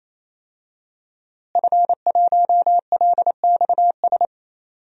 Morse code, heard as a keyed tone, about 28 wpm, 700 Hz. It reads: F1LXS
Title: DXCC-100 Call Signs - 5 Characters - Repeated 3x - 28wpm